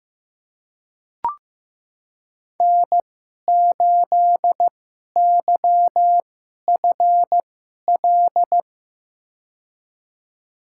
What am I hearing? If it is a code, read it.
Morse code, heard as a keyed tone, about 15 wpm, spaced as for 11 wpm, 700 Hz. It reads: N8YFL
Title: USA Call Signs - 5 Characters - 2x Character Spacing - 15wpm